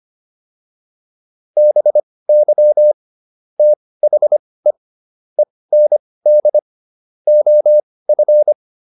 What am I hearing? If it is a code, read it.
Morse code, heard as a keyed tone, about 25 wpm, spaced as for 18 wpm, 600 Hz. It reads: BY THE END OF